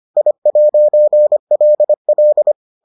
Morse code, heard as a keyed tone, about 25 wpm, 600 Hz. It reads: I'LL